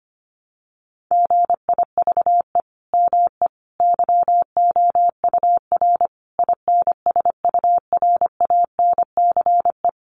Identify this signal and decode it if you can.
Morse code, heard as a keyed tone, about 25 wpm, 700 Hz. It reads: GIVE ME YOUR INSURANCE
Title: Sentences from Top 1100 Words 25wpm